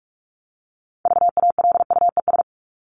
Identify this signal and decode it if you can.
Morse code, heard as a keyed tone, about 44 wpm, 700 Hz. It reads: VALUES
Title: ES:S Suffix - Top 100 Words with Base Words 45wpm